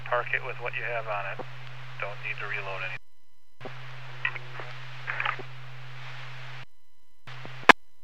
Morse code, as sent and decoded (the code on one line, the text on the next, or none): none